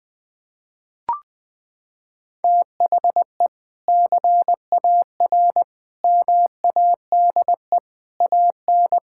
Morse code, heard as a keyed tone, about 21 wpm, 700 Hz. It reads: THE CAR MADE AN
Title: Sentences from Top 1000 Words - Review Part 6 - 20wpm